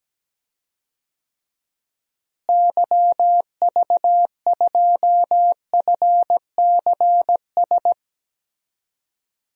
Morse code, heard as a keyed tone, about 17 wpm, 700 Hz. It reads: YV2FCS
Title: DXCC-100 Call Signs - 6 Characters - Repeated 2x - 17wpm